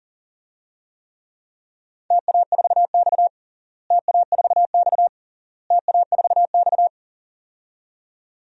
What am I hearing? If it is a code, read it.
Morse code, heard as a keyed tone, about 37 wpm, 700 Hz. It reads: TA4X TA4X TA4X
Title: DXCC-100 Call Signs - 4 Characters - Repeated 3x - 40wpm